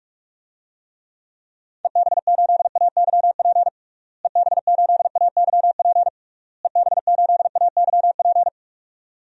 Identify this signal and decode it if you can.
Morse code, heard as a keyed tone, about 46 wpm, 700 Hz. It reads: EB8AYP EB8AYP EB8AYP
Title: DXCC-100 Call Signs - 6 Characters - Repeated 3x - 45wpm